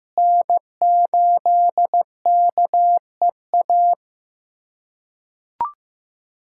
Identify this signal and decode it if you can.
Morse code, heard as a keyed tone, about 16 wpm, 700 Hz. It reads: N8KEA
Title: USA Callsign DE Callsign - 5 Characters - 15wpm